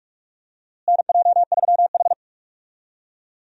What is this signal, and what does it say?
Morse code, heard as a keyed tone, about 45 wpm, 700 Hz. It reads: NJ3H
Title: USA Call Signs - 4 Characters - Repeated 3x - 45wpm